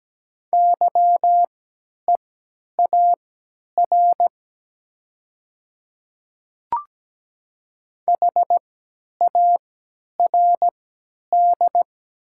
Morse code, heard as a keyed tone, about 17 wpm, spaced as for 10 wpm, 700 Hz. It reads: YEAR HARD
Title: 3x Character Spacing - Top 300 Words - 17wpm